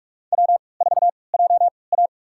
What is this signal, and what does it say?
Morse code, heard as a keyed tone, about 46 wpm, spaced as for 25 wpm, 700 Hz. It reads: W4JA
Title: USA Call Signs - 4 Characters - 3x Character Spacing - 45wpm